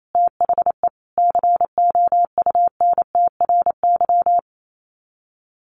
Morse code, heard as a keyed tone, about 28 wpm, 700 Hz. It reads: THE COUNTRY
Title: Sets of 5 Words Encore 28wpm